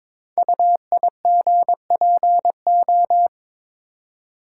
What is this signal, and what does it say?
Morse code, heard as a keyed tone, about 22 wpm, 700 Hz. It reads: UIGPO
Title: Code Groups - 5 Letters - 22wpm